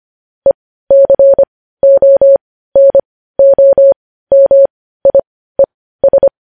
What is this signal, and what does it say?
Morse code, heard as a keyed tone, about 25 wpm, spaced as for 15 wpm, 550 Hz. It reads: ECONOMIES
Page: news headlines in morse code at 15 wpm